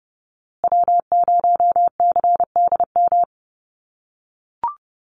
Morse code, heard as a keyed tone, about 30 wpm, 700 Hz. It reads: W0CDM
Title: USA Call Signs Repeated 3x 30wpm